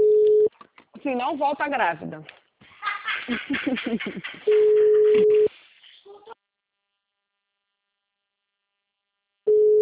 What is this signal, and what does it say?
Ringback tone (Europe, 425 Hz), with speech.